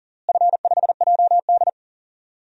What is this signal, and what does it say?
Morse code, heard as a keyed tone, about 40 wpm, 700 Hz. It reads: F5JD
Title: DXCC-100 Call Signs - 4 Characters - Repeated 3x - 40wpm